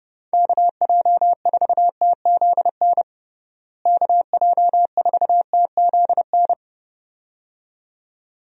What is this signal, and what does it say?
Morse code, heard as a keyed tone, about 30 wpm, 700 Hz. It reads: KJ4TZN KJ4TZN
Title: USA Call Signs - 6 Characters - Repeated 2x - 30wpm